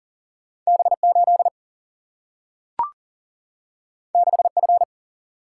Morse code, heard as a keyed tone, about 40 wpm, 700 Hz. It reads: B8 6F
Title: Single-Letter Single-Number 40wpm